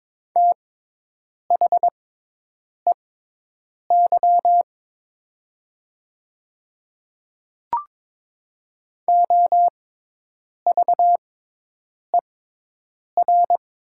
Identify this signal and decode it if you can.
Morse code, heard as a keyed tone, about 22 wpm, spaced as for 8 wpm, 700 Hz. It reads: THEY OVER
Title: 6x Character Spacing - Top 100 Words - 22wpm